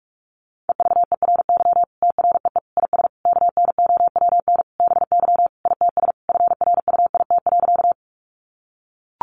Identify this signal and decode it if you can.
Morse code, heard as a keyed tone, about 44 wpm, 700 Hz. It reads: EVERY TREE IS KNOWN BY ITS FRUIT.
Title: Sentences from Top 1300 Words 45wpm